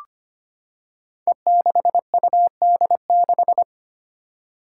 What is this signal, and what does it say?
Morse code, heard as a keyed tone, about 25 wpm, 700 Hz. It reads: E6UD6
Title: Five Challenging Letter-Number Combinations 25wpm